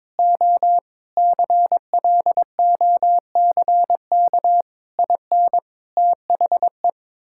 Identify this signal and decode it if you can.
Morse code, heard as a keyed tone, about 22 wpm, 700 Hz. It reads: O CLOCK IN THE
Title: Pangram Challenge v2 - 22wpm